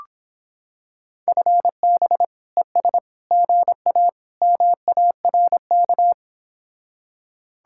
Morse code, heard as a keyed tone, about 26 wpm, 700 Hz. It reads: FB ES GA MARK